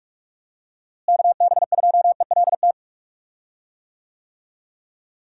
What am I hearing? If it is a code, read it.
Morse code, heard as a keyed tone, about 46 wpm, 700 Hz. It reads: KB2ELT
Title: USA Call Signs - 6 Characters - Repeated 2x - 45wpm